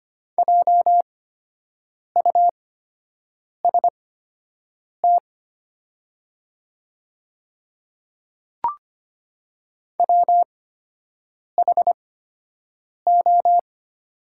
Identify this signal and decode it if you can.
Morse code, heard as a keyed tone, about 25 wpm, spaced as for 7 wpm, 690 Hz. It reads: JUST WHO
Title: 8x Character Spacing - Top 200 Words - 25wpm